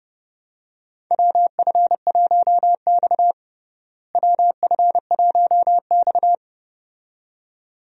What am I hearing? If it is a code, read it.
Morse code, heard as a keyed tone, about 31 wpm, 700 Hz. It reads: WF1X WF1X
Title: USA Call Signs - 4 Characters - Repeated 2x - 30wpm